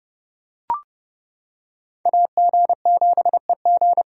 Morse code, heard as a keyed tone, about 28 wpm, 700 Hz. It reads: AG7EG